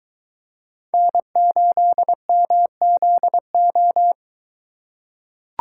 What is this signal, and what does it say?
Morse code, heard as a keyed tone, about 23 wpm, 700 Hz. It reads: N8MZO